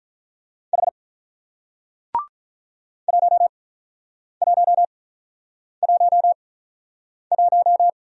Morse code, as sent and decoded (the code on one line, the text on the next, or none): .... / .---- .---- .---- .----
H 1111